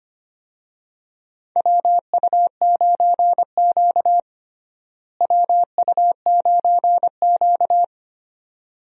Morse code, heard as a keyed tone, about 25 wpm, 700 Hz. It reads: WU9Q WU9Q
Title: USA Call Signs - 4 Characters - Repeated 2x - 25wpm